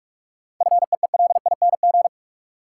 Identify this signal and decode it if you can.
Morse code, heard as a keyed tone, about 46 wpm, 700 Hz. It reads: FEELING
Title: Sentences from Top 1200 Words - New Word Drill Alt - 45wpm